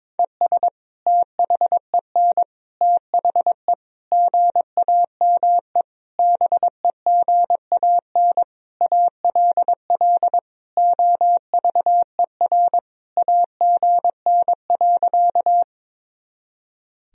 Morse code, heard as a keyed tone, about 22 wpm, 700 Hz. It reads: ES THEN THE GAME BEGAN ALL OVER AGN.
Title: Sentences from Top 500 Words and Abbreviations 22wpm